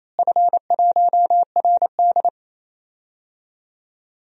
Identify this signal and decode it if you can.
Morse code, heard as a keyed tone, about 27 wpm, 700 Hz. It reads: F1RD
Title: DXCC-100 Call Signs - 4 Characters - Repeated 3x - 28wpm